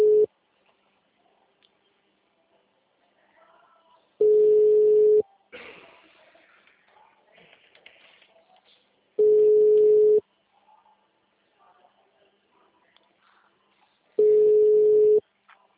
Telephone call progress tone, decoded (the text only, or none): ringback tone